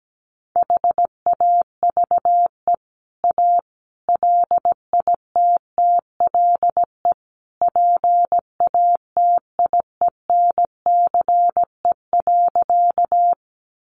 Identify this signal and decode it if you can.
Morse code, heard as a keyed tone, about 17 wpm, 700 Hz. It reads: HAVE A LITTLE PATIENCE.
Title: Sentences from Top 1300 Words 17wpm